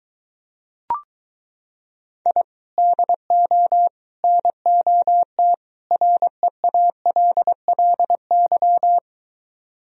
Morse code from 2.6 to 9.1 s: DO NOT REALLY

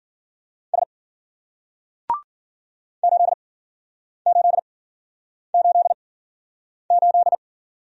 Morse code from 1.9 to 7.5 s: 8888

